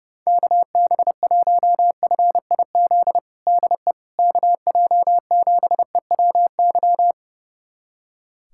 Morse code, heard as a keyed tone, about 30 wpm, 700 Hz. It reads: KB1FIZ DE KJ7EWY
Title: USA Callsign DE Callsign 30wpm